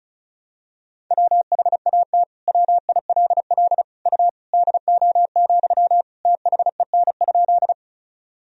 Morse code, heard as a keyed tone, about 34 wpm, 700 Hz. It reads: WHAT WILL U DO, THEN?
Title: Sentences from Top 100 Words and Abbreviations 35wpm